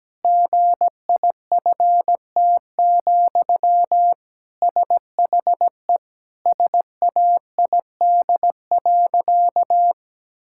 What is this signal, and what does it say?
Morse code, heard as a keyed tone, about 17 wpm, 700 Hz. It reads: GIFT, SHE SAID.
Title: Sentences from Top 1000 Words - Review Part 6 - 17wpm